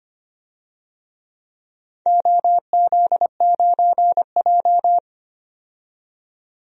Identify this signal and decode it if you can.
Morse code, heard as a keyed tone, about 25 wpm, 700 Hz. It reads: OZ9J
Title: DXCC-100 Call Signs - 4 Characters - 25wpm